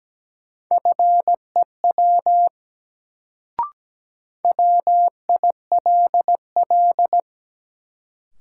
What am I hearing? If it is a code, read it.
Morse code, heard as a keyed tone, about 17 wpm, 700 Hz. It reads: FEW WILL
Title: Sets of 2 Words - New Word Drill - 17wpm